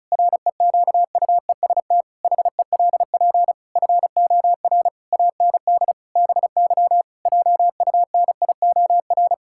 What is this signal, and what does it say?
Morse code, heard as a keyed tone, about 34 wpm, 700 Hz. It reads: REQUEST HELP FOR AND BY JUNIOR
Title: Pangram Challenge v2 - 35wpm